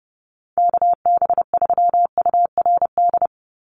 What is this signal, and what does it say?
Morse code, heard as a keyed tone, about 30 wpm, 700 Hz. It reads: KB3URD
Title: USA Call Signs 30wpm